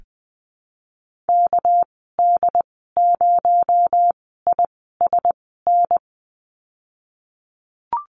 Morse code, heard as a keyed tone, about 20 wpm, spaced as for 15 wpm, 700 Hz. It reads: KD0ISN
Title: 231 Lesson 37g - Letter x - Call Signs 20wpm